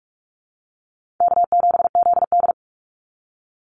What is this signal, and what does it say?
Morse code, heard as a keyed tone, about 44 wpm, 700 Hz. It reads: K7ZD
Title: Easy Call Signs - 45wpm